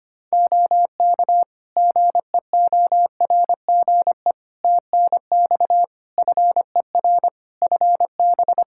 Morse code, heard as a keyed tone, about 25 wpm, 700 Hz. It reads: OK GEORGE TNX FER FB
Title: K1USN SST QSO Practice 25wpm